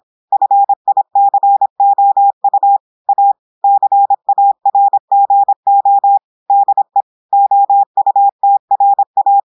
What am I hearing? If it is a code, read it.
Morse code, heard as a keyed tone, about 26 wpm, 800 Hz. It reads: FICOU A CARGO DE OUTRA
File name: audios / portugais / portugais texte en clair 1 vitesse 26 mots minutes